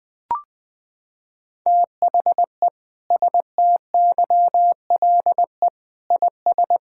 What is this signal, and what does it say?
Morse code, heard as a keyed tone, about 20 wpm, 700 Hz. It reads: THE STYLE IS
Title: Sentences from Top 1000 Words - Review Part 5 - 20wpm